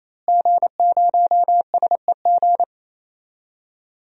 Morse code, heard as a keyed tone, about 28 wpm, 700 Hz. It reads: G0SEG
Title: DXCC-100 Call Signs - 5 Characters - Repeated 3x - 28wpm